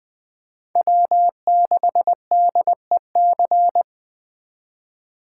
Morse code, heard as a keyed tone, about 20 wpm, 700 Hz. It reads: W6DEC
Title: USA Call Signs - 5 Characters - 3x Character Spacing - 20wpm